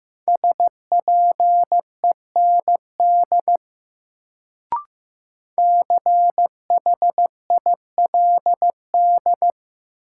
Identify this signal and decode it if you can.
Morse code, heard as a keyed tone, about 16 wpm, 700 Hz. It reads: SPEND CHILD
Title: Sentences from Top 800 Words - New Word Drill - 15wpm